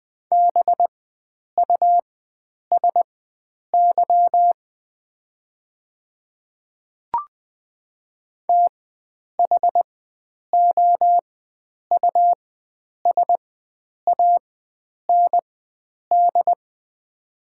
Morse code, heard as a keyed tone, about 20 wpm, spaced as for 9 wpm, 690 Hz. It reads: BUSY THOUSAND
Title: 4x Character Spacing - Top 500 Words - 20wpm